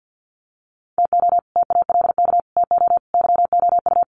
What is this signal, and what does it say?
Morse code, heard as a keyed tone, about 50 wpm, 700 Hz. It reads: TO TALK TO YOU
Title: Sets of 4 Words 50wpm